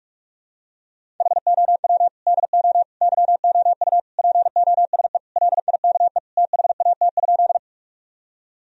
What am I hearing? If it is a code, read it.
Morse code, heard as a keyed tone, about 46 wpm, 700 Hz. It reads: HOW DO YOU POSE LIKE THAT?
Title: Sentences from Top 1000 Words - Review Part 2 - 45wpm